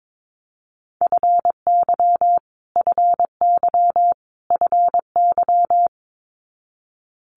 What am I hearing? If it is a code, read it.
Morse code, heard as a keyed tone, about 22 wpm, 700 Hz. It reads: FY FY FY
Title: Multi-letter Phonemes Repeated 3x 22wpm